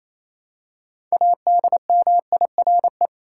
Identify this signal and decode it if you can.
Morse code, heard as a keyed tone, about 28 wpm, 700 Hz. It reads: ADMIRE
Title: Six Letter Words - 28wpm